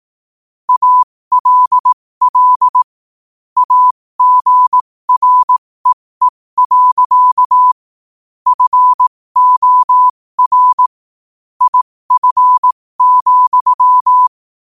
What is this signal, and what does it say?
Morse code, heard as a keyed tone, about 18 wpm, spaced as for 15 wpm, 1000 Hz. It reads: ALL AGREE. FOR IF,
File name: qotdfiles / 15wpm